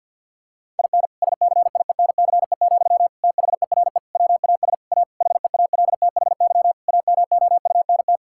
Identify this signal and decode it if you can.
Morse code, heard as a keyed tone, about 50 wpm, 700 Hz. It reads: IN SCIENCE, THERE WAS A HEALTHY AMOUNT
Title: Sentences from Top 1000 Words - Review Part 5 - 50wpm